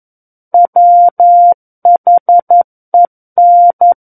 Morse code, heard as a keyed tone, about 11 wpm, 700 Hz. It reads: WHEN
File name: Book One / 02-Chapter 2 - 11 WPM